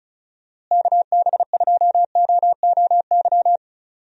Morse code, heard as a keyed tone, about 36 wpm, 700 Hz. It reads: KB2OOY
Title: USA Call Signs - 6 Characters - Repeated 2x - 35wpm